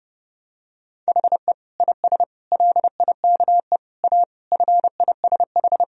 Morse code, heard as a keyed tone, about 30 wpm, 700 Hz. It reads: HE IS LIKE A FISH
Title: Sentences from Top 1000 Words - Review Part 1 - 30wpm